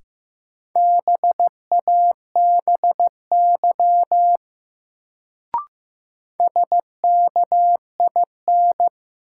Morse code, heard as a keyed tone, about 15 wpm, 700 Hz. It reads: BABY SKIN
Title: Sentences from Top 1300 Words - New Word Drill Alt - 15wpm